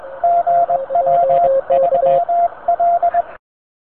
Morse code, heard as a keyed tone, about 21 wpm, 690 Hz. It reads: GW3L